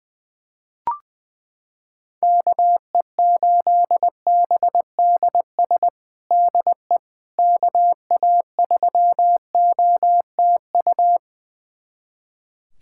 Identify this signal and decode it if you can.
Morse code, heard as a keyed tone, about 20 wpm, 700 Hz. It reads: KE8BDS DE KA3OTU
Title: USA Callsign DE Callsign 20wpm